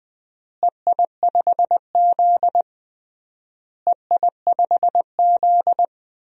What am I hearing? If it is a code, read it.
Morse code, heard as a keyed tone, about 20 wpm, 690 Hz. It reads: EI5Z EI5Z
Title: DXCC-100 Call Signs - 4 Characters - Repeated 3x - 20wpm